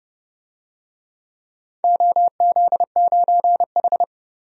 Morse code, heard as a keyed tone, about 29 wpm, 700 Hz. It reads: OZ9H
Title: DXCC-100 Call Signs - 4 Characters - 30wpm